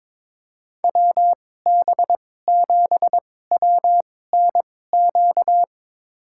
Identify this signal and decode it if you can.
Morse code, heard as a keyed tone, about 22 wpm, spaced as for 16 wpm, 700 Hz. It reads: WB7WNQ
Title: USA Call Signs - 6 Characters - 2x Character Spacing - 22wpm